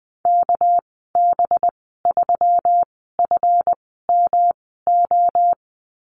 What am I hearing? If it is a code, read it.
Morse code, heard as a keyed tone, about 20 wpm, spaced as for 15 wpm, 700 Hz. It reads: KB3FMO
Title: 200 Lesson 32g - Letter k - Call Signs 20wpm